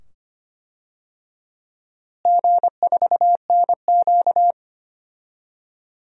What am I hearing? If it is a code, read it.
Morse code, heard as a keyed tone, about 25 wpm, 700 Hz. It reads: G4NQ
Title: DXCC-100 Call Signs - 4 Characters - Repeated 2x - 25wpm